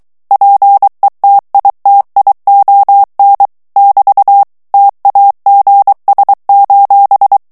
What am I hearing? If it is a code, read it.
Morse code, heard as a keyed tone, about 23 wpm, 800 Hz. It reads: PETITION <BT> TAGS:
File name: ANGLAIS texte clair 3 vitesse 22 mots minute